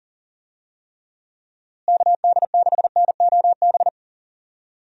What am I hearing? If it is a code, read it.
Morse code, heard as a keyed tone, about 39 wpm, 700 Hz. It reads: KD6NOB
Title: USA Call Signs - 6 Characters - Repeated 3x - 40wpm